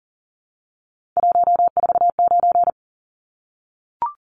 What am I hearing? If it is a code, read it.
Morse code, heard as a keyed tone, about 40 wpm, 700 Hz. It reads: 149
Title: Numbers - 3-Digit 40wpm